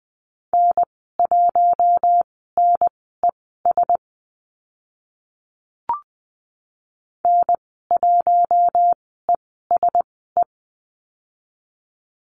Morse code, heard as a keyed tone, about 20 wpm, spaced as for 14 wpm, 700 Hz. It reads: N1NES N1ESE
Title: 042 Lesson 8g - Number 1 - Call Signs 20wpm